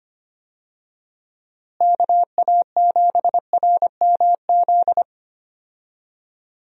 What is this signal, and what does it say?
Morse code, heard as a keyed tone, about 25 wpm, 700 Hz. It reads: KA7RMZ
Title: USA Call Signs - 6 Characters - Repeated 2x - 25wpm